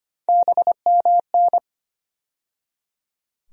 Morse code, heard as a keyed tone, about 25 wpm, 700 Hz. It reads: BMN
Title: USA Call Sign Postfixes - 3 Letters 25wpm